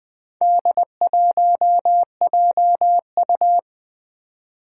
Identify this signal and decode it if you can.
Morse code, heard as a keyed tone, about 20 wpm, 700 Hz. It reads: D1JU
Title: Four Challenging Letter-Number Combinations 20wpm